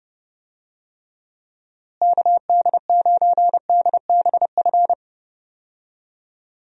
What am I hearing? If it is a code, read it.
Morse code, heard as a keyed tone, about 30 wpm, 700 Hz. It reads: KD9DBF
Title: USA Call Signs - 6 Characters - Repeated 2x - 30wpm